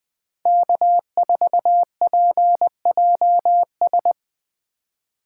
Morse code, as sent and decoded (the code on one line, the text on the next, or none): -.- ....- .--. .--- ...
K4PJS